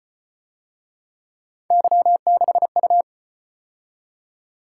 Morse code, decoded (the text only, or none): Y6U